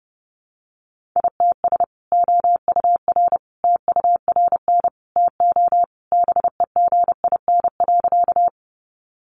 Morse code, heard as a keyed tone, about 30 wpm, 700 Hz. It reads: ITS OUR TURN TO BEGIN.
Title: Sentences from Top 600 Words 30wpm